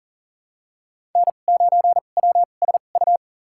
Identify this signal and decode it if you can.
Morse code, heard as a keyed tone, about 40 wpm, spaced as for 29 wpm, 700 Hz. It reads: N9WSU
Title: USA Call Signs - 5 Characters - 2x Character Spacing - 40wpm